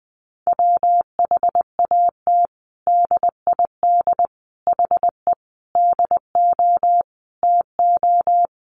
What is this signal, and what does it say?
Morse code, heard as a keyed tone, about 19 wpm, 700 Hz. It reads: WHAT DID HE DO TO
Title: Sentences from Top 100 Words 20wpm